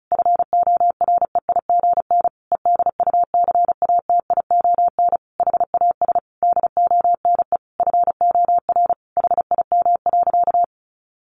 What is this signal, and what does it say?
Morse code, heard as a keyed tone, about 34 wpm, 700 Hz. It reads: FOREIGN EDUCATION HAS DONE FOR HIM.
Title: Sentences from Top 1400 Words 35wpm